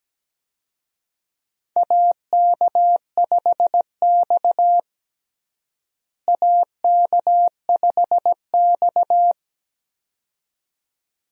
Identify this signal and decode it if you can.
Morse code, heard as a keyed tone, about 17 wpm, 700 Hz. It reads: AK5X AK5X
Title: USA Call Signs - 4 Characters - Repeated 2x - 17wpm